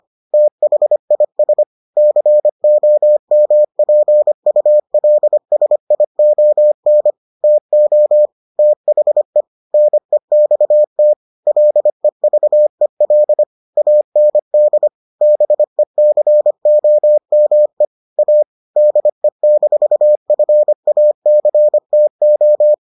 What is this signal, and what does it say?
Morse code, heard as a keyed tone, about 25 wpm, 600 Hz. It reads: THIS COMPULSION TO THE NEXT LEVEL AND BECOME A DE-FACTO